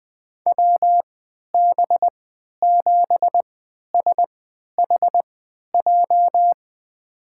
Morse code, heard as a keyed tone, about 20 wpm, spaced as for 11 wpm, 700 Hz. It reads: WB7SHJ
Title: USA Call Signs - 6 Characters - 3x Character Spacing - 20wpm